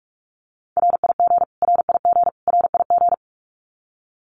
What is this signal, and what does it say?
Morse code, heard as a keyed tone, about 44 wpm, 700 Hz. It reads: RIG RIG RIG
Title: Abbreviations Repeated 3x - 45wpm